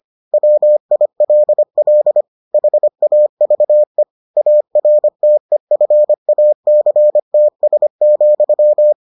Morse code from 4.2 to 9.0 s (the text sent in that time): ARTEFACTS,